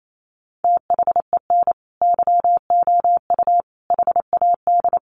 Morse code, heard as a keyed tone, about 28 wpm, 700 Hz. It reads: THEN YOU HAD
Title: Sentences from Top 1300 Words 28wpm